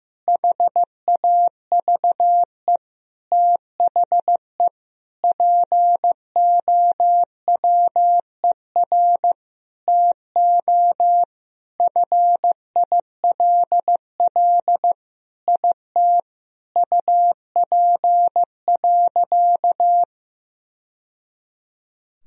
Morse code, heard as a keyed tone, about 16 wpm, 700 Hz. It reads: HAVE THE POWER TO FILL IT UP.
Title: Sentences from Top 1000 Words - Review Part 3 - 15wpm